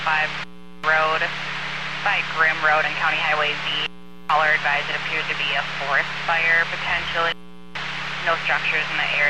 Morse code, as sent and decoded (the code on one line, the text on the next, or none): none